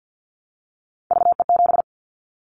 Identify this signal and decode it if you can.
Morse code, heard as a keyed tone, about 50 wpm, 700 Hz. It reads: VE7
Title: Callsign Prefixes for Top 25 DXCC - Repeated 3x - 50wpm